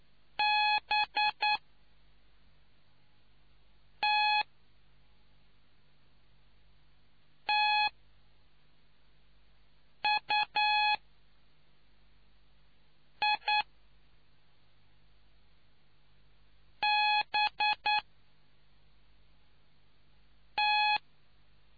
Morse code, heard as a keyed tone, about 9 wpm, spaced as for 3 wpm, 840 Hz. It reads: BTTUIBT